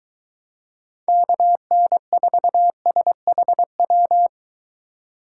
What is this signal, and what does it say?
Morse code, heard as a keyed tone, about 23 wpm, 700 Hz. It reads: KN4SHW